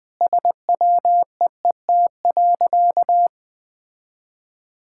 Morse code, heard as a keyed tone, about 21 wpm, 700 Hz. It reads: SWEET.
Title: Sentences with 3-Word Phrases - Spoken Beginning - 20wpm